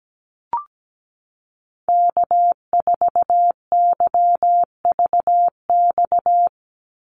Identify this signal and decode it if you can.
Morse code, heard as a keyed tone, about 17 wpm, 700 Hz. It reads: K4YVX
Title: USA Call Signs 17wpm